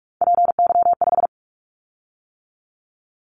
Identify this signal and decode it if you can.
Morse code, heard as a keyed tone, about 46 wpm, 700 Hz. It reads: PY5
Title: Callsign Prefixes for Top 50 DXCC - 45wpm